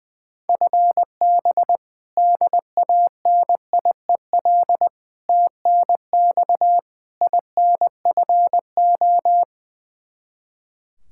Morse code, heard as a keyed tone, about 20 wpm, 700 Hz. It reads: FB DANIEL TNX INFO
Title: Practice QSO - Exchange 3 - More Info - 20wpm